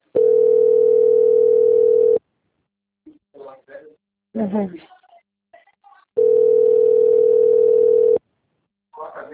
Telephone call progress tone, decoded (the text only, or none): ringback tone